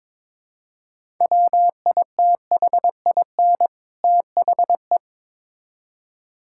Morse code, heard as a keyed tone, about 22 wpm, 700 Hz. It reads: WITHIN THE
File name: Build A Phrase - 2-Word Phrases - 22wpm